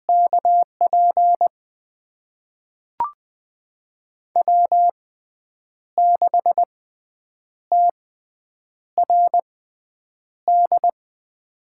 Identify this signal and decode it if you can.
Morse code, heard as a keyed tone, about 20 wpm, spaced as for 8 wpm, 700 Hz. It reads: KP W6TRD
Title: USA Call Signs - 6x Character Spacing - 20wpm